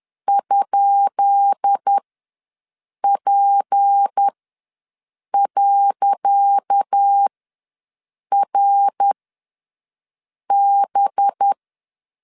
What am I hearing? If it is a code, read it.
Morse code, heard as a keyed tone, about 10 wpm, spaced as for 6 wpm, 800 Hz. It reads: ?P.RB
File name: LEZIONI / 10 SIGNS / 10C DOT ws 10wpm 4213